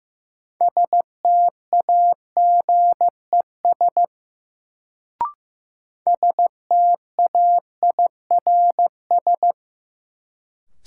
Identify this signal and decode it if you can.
Morse code, heard as a keyed tone, about 15 wpm, 700 Hz. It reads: STAGES STAIRS
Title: Sets of 5 Words Encore - New Word Drill - 15wpm